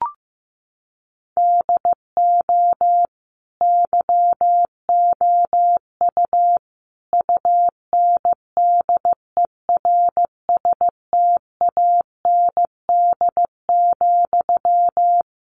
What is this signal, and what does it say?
Morse code, heard as a keyed tone, about 15 wpm, 700 Hz. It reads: DO YOU UNDERSTAND,
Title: Sentences from Top 1200 Words 15wpm